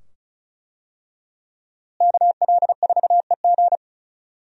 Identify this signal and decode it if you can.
Morse code, heard as a keyed tone, about 36 wpm, 700 Hz. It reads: KL4EG
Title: USA Call Signs - 5 Characters - Repeated 2x - 35wpm